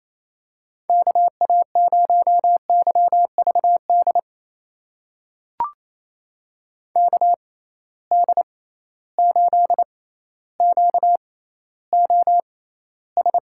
Morse code, heard as a keyed tone, about 28 wpm, spaced as for 14 wpm, 700 Hz. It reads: KA0YVD KD8QOS
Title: USA Call Signs - 6 Characters - 6x Character Spacing - 28wpm